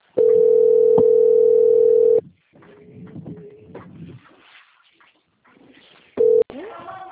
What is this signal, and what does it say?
Ringback tone (North America), 440 + 480 Hz.
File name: out- - - - - .19883